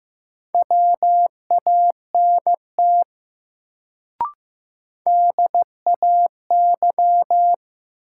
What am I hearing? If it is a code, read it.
Morse code, heard as a keyed tone, about 15 wpm, 700 Hz. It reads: WANT DAY
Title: Sentences from Top 200 Words - New Word Drill - 15wpm